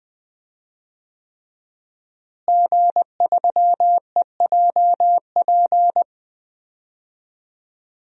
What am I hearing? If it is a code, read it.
Morse code, heard as a keyed tone, about 21 wpm, 700 Hz. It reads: G3EJP
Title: DXCC-100 Call Signs - 5 Characters - Repeated 3x - 20wpm